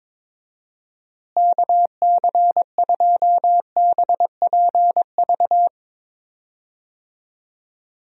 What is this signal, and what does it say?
Morse code, heard as a keyed tone, about 22 wpm, 700 Hz. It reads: KC2BPV
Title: USA Call Signs - 6 Characters - Repeated 2x - 22wpm